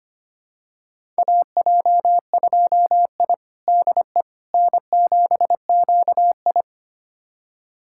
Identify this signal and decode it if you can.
Morse code, heard as a keyed tone, about 25 wpm, 700 Hz. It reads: AJ2I DE N7QI
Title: USA Callsign DE Callsign - 4 Characters - 25wpm